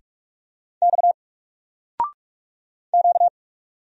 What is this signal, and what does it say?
Morse code, heard as a keyed tone, about 44 wpm, 700 Hz. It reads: X Q
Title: Single Letter-Number 45wpm